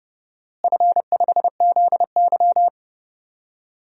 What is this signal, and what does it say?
Morse code, heard as a keyed tone, about 30 wpm, 700 Hz. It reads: F5ZY
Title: DXCC-100 Call Signs - 4 Characters - Repeated 2x - 30wpm